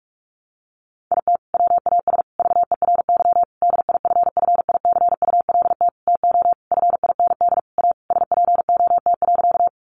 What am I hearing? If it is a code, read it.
Morse code, heard as a keyed tone, about 44 wpm, 700 Hz. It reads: IT WAS VERY DIFFICULT TO FIND A SPOT.
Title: Sentences from Top 1100 Words 45wpm